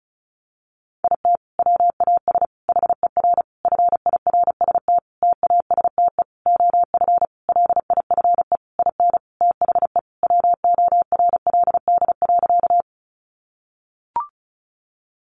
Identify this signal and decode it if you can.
Morse code, heard as a keyed tone, about 34 wpm, 700 Hz. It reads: IT WAS HER FIRST TASTE OF LIFE IN THE WORLD.
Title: Sentences from Top 1100 Words 35wpm